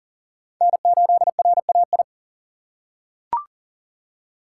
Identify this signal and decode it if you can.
Morse code, heard as a keyed tone, about 40 wpm, 700 Hz. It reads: N8RAI
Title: USA Call Signs - 5 Characters - 6x Character Spacing - 40wpm